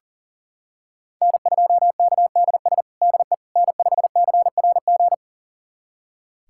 Morse code, heard as a keyed tone, about 37 wpm, 700 Hz. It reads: N2KDS DE N5CRG
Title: USA Callsign DE Callsign - 5 Characters - 40wpm